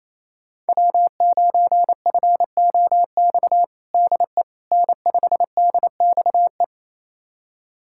Morse code, heard as a keyed tone, about 28 wpm, 700 Hz. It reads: W9FOX DE N5DXE
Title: USA Callsign DE Callsign - 5 Characters - 28wpm